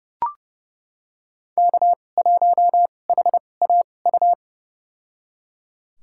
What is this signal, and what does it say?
Morse code, heard as a keyed tone, about 30 wpm, spaced as for 22 wpm, 700 Hz. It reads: K1HAU
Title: USA Call Signs - 5 Characters - 2x Character Spacing - 30wpm